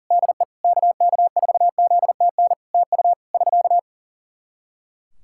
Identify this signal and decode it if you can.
Morse code, heard as a keyed tone, about 40 wpm, 700 Hz. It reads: DE KK4ZTN TU <SK>
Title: Practice QSO - Exchange 4 - Ending - 40wpm